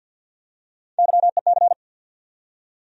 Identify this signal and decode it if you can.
Morse code, heard as a keyed tone, about 52 wpm, 690 Hz. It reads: YEC
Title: USA Call Sign Postfixes - 3 Letters 50wpm